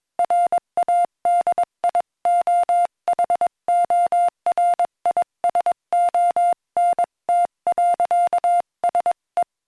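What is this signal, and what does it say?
Morse code, heard as a keyed tone, about 21 wpm, 700 Hz. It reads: RADIOHORISONT.SE
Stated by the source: Morse code taught in a Swedish lesson on learning telegraphy, from QZJ and Utesändaren.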